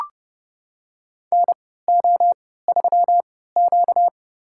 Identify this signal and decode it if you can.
Morse code, heard as a keyed tone, about 30 wpm, spaced as for 17 wpm, 700 Hz. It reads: NO3Q
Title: USA Call Signs - 4 Characters - 3x Character Spacing - 30wpm